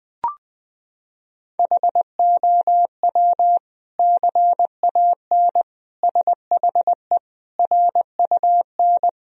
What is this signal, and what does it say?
Morse code, heard as a keyed tone, about 20 wpm, 700 Hz. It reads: HOW CAN SHE RUN
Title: Sentences from Top 1000 Words - Review Part 2 - 20wpm